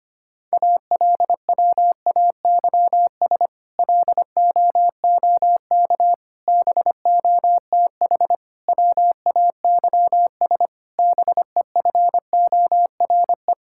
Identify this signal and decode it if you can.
Morse code, heard as a keyed tone, about 25 wpm, 700 Hz. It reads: ALWAYS LOOK BOTH WAYS BEFORE
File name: practice / 25wpm / Difficult Sentences with Hints - 25wpm